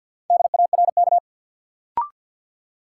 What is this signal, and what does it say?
Morse code, heard as a keyed tone, about 53 wpm, 700 Hz. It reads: DARK
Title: Sentences from Top 500 Words - New Word Drill - 50wpm